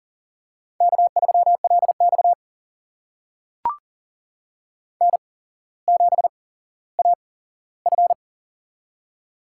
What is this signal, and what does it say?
Morse code, heard as a keyed tone, about 40 wpm, spaced as for 17 wpm, 700 Hz. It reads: K3LX N7AF
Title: USA Call Signs - 4 Characters - 8x Character Spacing - 40wpm